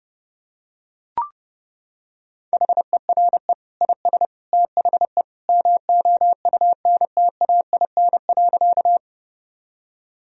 Morse code, heard as a keyed tone, about 31 wpm, 700 Hz. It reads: HERE IS THE MOUNTAIN.
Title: Sentences from Top 1000 Words - Review Part 1 - 30wpm